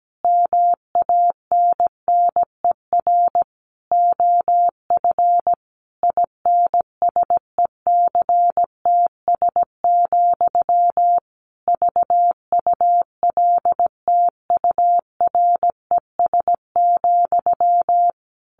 Morse code, heard as a keyed tone, about 17 wpm, 700 Hz. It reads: MANNER OF INSECTS, VULTURES,
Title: A Sound Of Thunder 17wpm